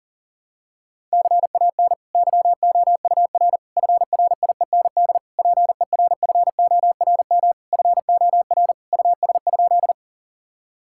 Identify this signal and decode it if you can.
Morse code, heard as a keyed tone, about 40 wpm, 690 Hz. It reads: CAN YOUR FRIEND PERFORM FOR US?
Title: Sentences from Top 1000 Words - Review Part 5 - 40wpm